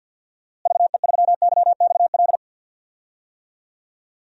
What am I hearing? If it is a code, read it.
Morse code, heard as a keyed tone, about 51 wpm, 700 Hz. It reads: VE3YXL
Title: DXCC-100 Call Signs - 6 Characters - Repeated 2x - 50wpm